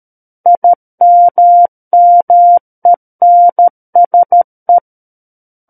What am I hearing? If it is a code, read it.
Morse code, heard as a keyed tone, about 13 wpm, 700 Hz. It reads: IMMENSE
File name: Book One / 04-Chapter 4 - 13 WPM